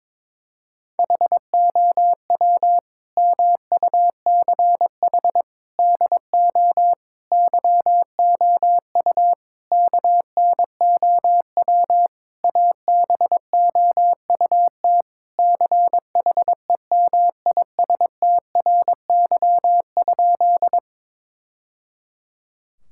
Morse code, heard as a keyed tone, about 22 wpm, 700 Hz. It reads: HOW MUCH DO YOU KNOW ABOUT CHEMISTRY?
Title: Sentences from Top 1000 Words - Review Part 6 - 22wpm